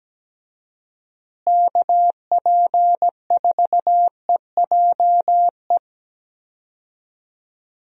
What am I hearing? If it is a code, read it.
Morse code, heard as a keyed tone, about 17 wpm, 700 Hz. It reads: KP4EJE
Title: DXCC-100 Call Signs - 6 Characters - Repeated 3x - 17wpm